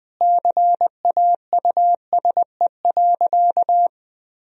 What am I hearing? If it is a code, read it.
Morse code, heard as a keyed tone, about 20 wpm, 700 Hz. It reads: CAUSE.
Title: Sentences from Top 1000 Words - Review Part 2 - 20wpm